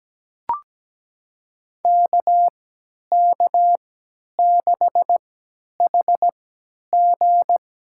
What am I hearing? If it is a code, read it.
Morse code, heard as a keyed tone, about 17 wpm, spaced as for 10 wpm, 700 Hz. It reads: KK6HG
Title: USA Call Signs - 3x Character Spacing - 17wpm